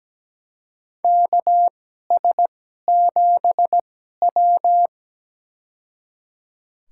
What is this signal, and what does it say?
Morse code, heard as a keyed tone, about 17 wpm, spaced as for 12 wpm, 700 Hz. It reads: KS7W
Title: USA Call Signs - 4 Characters - 2x Character Spacing - 17wpm